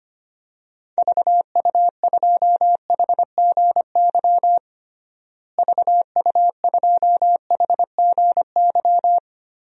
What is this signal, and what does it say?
Morse code, heard as a keyed tone, about 25 wpm, 700 Hz. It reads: VU2HGY VU2HGY
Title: DXCC-100 Call Signs - 6 Characters - Repeated 3x - 25wpm